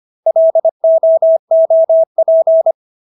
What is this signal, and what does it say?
Morse code, heard as a keyed tone, about 25 wpm, 650 Hz. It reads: LOOP